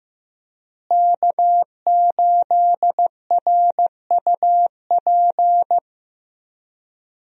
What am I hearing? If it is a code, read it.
Morse code, heard as a keyed tone, about 15 wpm, 700 Hz. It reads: K8RUP
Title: USA Call Signs - 5 Characters - Repeated 3x - 15wpm